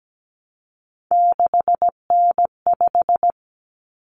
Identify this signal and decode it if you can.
Morse code, heard as a keyed tone, about 17 wpm, 700 Hz. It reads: 6N5
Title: Common Cut Numbers - 3-Digit 17wpm